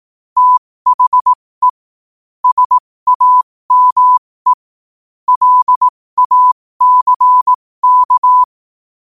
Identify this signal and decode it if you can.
Morse code, heard as a keyed tone, about 18 wpm, spaced as for 15 wpm, 1000 Hz. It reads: THE SAME LACK
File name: qotdfiles / 15wpm